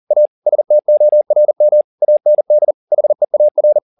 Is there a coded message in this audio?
Yes: Morse code from 0.1 to 3.8 s, heard as a keyed tone, about 40 wpm, 600 Hz. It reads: A STORM AND HEAR